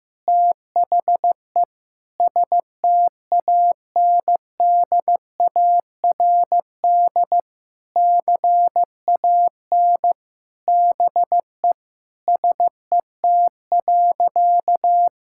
Morse code, heard as a keyed tone, about 16 wpm, 700 Hz. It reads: THE STANDARD CAN BE SET.
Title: Sentences from Top 1000 Words - Review Part 6 - 15wpm